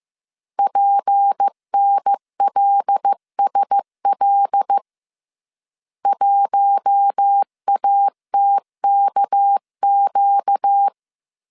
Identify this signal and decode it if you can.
Morse code, heard as a keyed tone, about 15 wpm, 800 Hz. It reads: PNLSL 1ATKQ